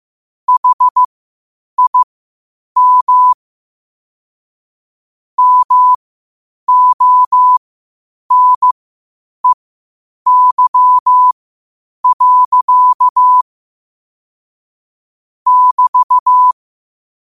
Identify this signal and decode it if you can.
Morse code, heard as a keyed tone, about 15 wpm, spaced as for 8 wpm, 1000 Hz. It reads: HIM MONEY. <BT>
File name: qotdfiles / 8wpm